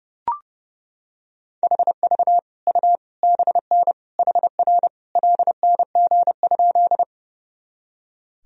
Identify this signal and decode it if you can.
Morse code, heard as a keyed tone, about 30 wpm, 700 Hz. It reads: HV U BN HR LNG?
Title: Sentences from Top 200 Words and Abbreviations 30wpm